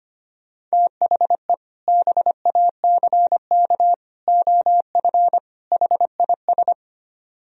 Morse code, heard as a keyed tone, about 32 wpm, 700 Hz. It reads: THE BACK OF HIS